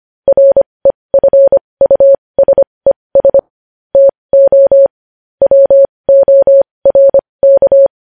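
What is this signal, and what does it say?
Morse code, heard as a keyed tone, about 25 wpm, spaced as for 20 wpm, 550 Hz. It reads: REFUSES TO WORK
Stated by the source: Morse code sending news headlines at 20 wpm.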